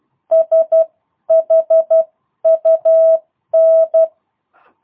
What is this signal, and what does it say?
Morse code, heard as a keyed tone, about 12 wpm, spaced as for 10 wpm, 650 Hz. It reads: SHUN
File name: jh8cbh / 2022cw